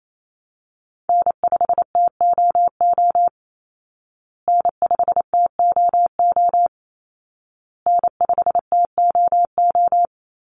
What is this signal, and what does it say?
Morse code, heard as a keyed tone, about 28 wpm, 700 Hz. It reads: N5TOO N5TOO N5TOO
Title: Call Signs Repeated 2x - 28wpm